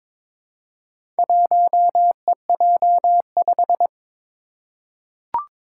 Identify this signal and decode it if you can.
Morse code, heard as a keyed tone, about 22 wpm, 700 Hz. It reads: 1EJ5
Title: Four Challenging Letter-Number Combinations 22wpm